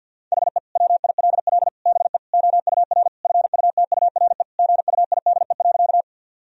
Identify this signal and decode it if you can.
Morse code, heard as a keyed tone, about 51 wpm, 700 Hz. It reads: HE WILL BE OUR FUTURE GUIDE.
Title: Sentences from Top 1000 Words - Review Part 3 - 50wpm